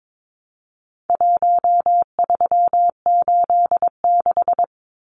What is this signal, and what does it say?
Morse code, heard as a keyed tone, about 22 wpm, 700 Hz. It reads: 1386
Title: Numbers - Contest Style - 4-Digit 22wpm